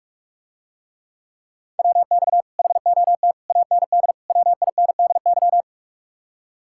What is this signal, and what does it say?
Morse code, heard as a keyed tone, about 48 wpm, 700 Hz. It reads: WX HOT AND WINDY